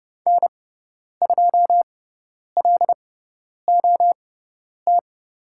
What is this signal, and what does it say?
Morse code, heard as a keyed tone, about 30 wpm, spaced as for 10 wpm, 700 Hz. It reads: N2LOT
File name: Lesson 18g - Letter m - Call Signs 30wpm